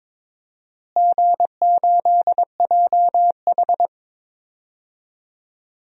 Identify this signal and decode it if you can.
Morse code, heard as a keyed tone, about 22 wpm, 700 Hz. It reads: G8JH
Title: DXCC-100 Call Signs - 4 Characters - Repeated 3x - 22wpm